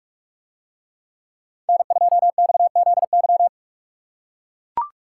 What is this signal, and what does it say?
Morse code, heard as a keyed tone, about 43 wpm, 700 Hz. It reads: N2XZY